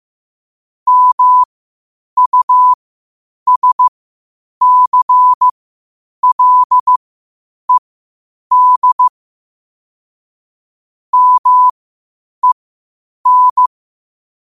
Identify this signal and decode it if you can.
Morse code, heard as a keyed tone, about 15 wpm, spaced as for 8 wpm, 1000 Hz. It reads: MUSCLED MEN
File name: qotdfiles / 8wpm